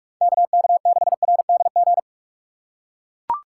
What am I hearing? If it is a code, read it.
Morse code, heard as a keyed tone, about 42 wpm, 700 Hz. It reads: KK6RDG